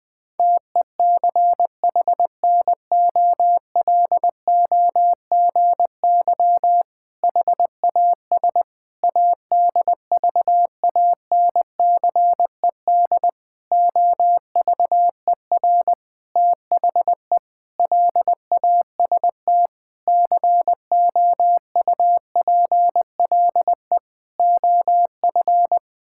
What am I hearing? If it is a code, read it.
Morse code, heard as a keyed tone, about 20 wpm, 700 Hz. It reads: TECHNOLOGY HAS ADVANCED OVER THE LAST COUPLE OF
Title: Sentences with 6-Word Phrases - Spoken Beginning - 20wpm